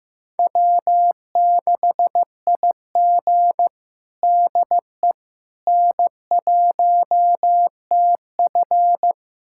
Morse code, heard as a keyed tone, about 15 wpm, 700 Hz. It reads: W6IG DE N1TF
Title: USA Callsign DE Callsign - 4 Characters - 15wpm